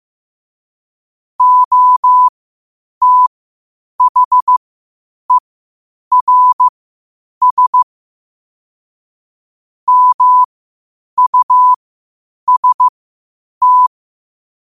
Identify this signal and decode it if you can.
Morse code, heard as a keyed tone, about 15 wpm, spaced as for 8 wpm, 1000 Hz. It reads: OTHERS MUST